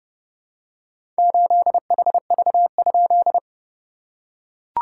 Morse code, heard as a keyed tone, about 30 wpm, 690 Hz. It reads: 8HV?
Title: SOTA QSO Practice 30wpm